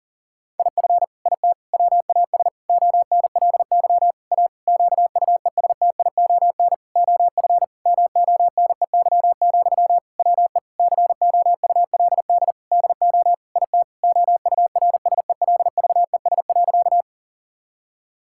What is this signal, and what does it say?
Morse code, heard as a keyed tone, about 40 wpm, 700 Hz. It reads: IF IT WAS ONLY A QUESTION OF MONEY, WE COULD DO IT OURSELVES.
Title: Sentences from Top 1600 Words 40wpm